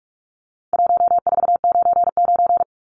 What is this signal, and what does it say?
Morse code, heard as a keyed tone, about 44 wpm, 700 Hz. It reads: 1499
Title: Numbers - 4-Digit 45wpm